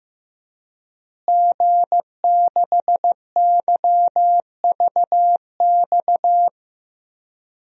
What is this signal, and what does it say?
Morse code, heard as a keyed tone, about 15 wpm, 700 Hz. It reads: G6YVX
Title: DXCC-100 Call Signs - 5 Characters - Repeated 3x - 15wpm